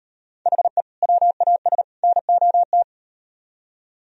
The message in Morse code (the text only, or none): HE WAS NOT